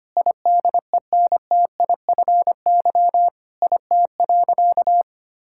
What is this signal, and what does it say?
Morse code, heard as a keyed tone, about 25 wpm, 700 Hz. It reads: IDENTIFY IT.
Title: Sentences from Top 1000 Words - Review Part 6 - 25wpm